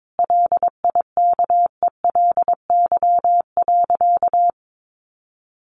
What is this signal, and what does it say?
Morse code, heard as a keyed tone, about 22 wpm, 700 Hz. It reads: LIKELY.
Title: Sentences from Top 600 Words 22wpm